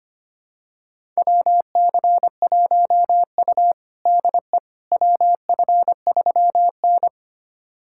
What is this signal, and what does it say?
Morse code, heard as a keyed tone, about 25 wpm, 700 Hz. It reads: WC1U DE WF3N
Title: USA Callsign DE Callsign - 4 Characters - 25wpm